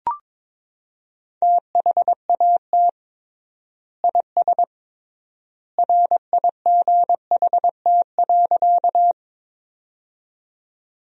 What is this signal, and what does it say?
Morse code, heard as a keyed tone, about 22 wpm, 700 Hz. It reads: THAT IS RIGHT.
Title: 3x Word-Spacing - Sentences from Top 200 Words - 22wpm